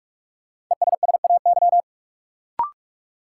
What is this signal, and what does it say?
Morse code, heard as a keyed tone, about 46 wpm, 700 Hz. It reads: ESSAY
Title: Sentences from Top 1500 Words - New Word Drill Alt - 45wpm